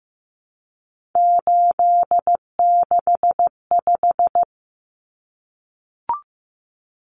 Morse code, heard as a keyed tone, about 15 wpm, 700 Hz. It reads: 865
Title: Common Cut Numbers - 3-Digit 15wpm